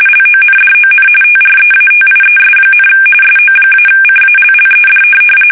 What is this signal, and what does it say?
RTTY, about 45.45 baud. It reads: LONDON TREATIES, SO-CALLED
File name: fsk850 / USS-Massachusetts